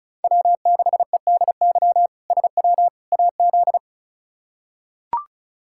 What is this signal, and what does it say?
Morse code, heard as a keyed tone, about 36 wpm, 700 Hz. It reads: W6EDY SW AZ
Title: K1USN SST QSO Practice - 35wpm